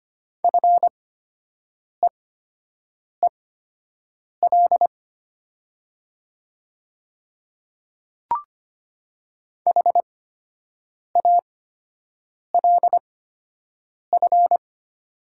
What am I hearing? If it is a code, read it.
Morse code, heard as a keyed tone, about 25 wpm, spaced as for 7 wpm, 700 Hz. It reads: FEEL HALF
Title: 8x Character Spacing - Top 500 Words - 25wpm